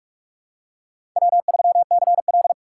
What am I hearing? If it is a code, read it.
Morse code, heard as a keyed tone, about 43 wpm, 700 Hz. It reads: W3CL
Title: USA Call Signs - 4 Characters - 8x Character Spacing - 45wpm